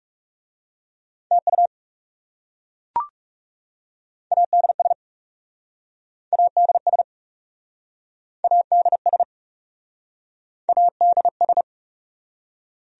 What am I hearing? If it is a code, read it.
Morse code, heard as a keyed tone, about 38 wpm, 700 Hz. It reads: TU ADS ADS ADS ADS